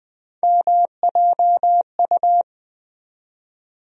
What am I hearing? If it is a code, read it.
Morse code, heard as a keyed tone, about 20 wpm, 700 Hz. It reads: MJU
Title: USA Call Sign Postfixes - 3 Letters 20wpm